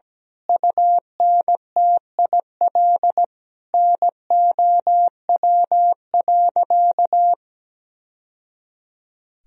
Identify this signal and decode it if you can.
Morse code, heard as a keyed tone, about 17 wpm, 700 Hz. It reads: UNTIL NOW.
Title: Sentences from Top 1000 Words - Review Part 3 - 17wpm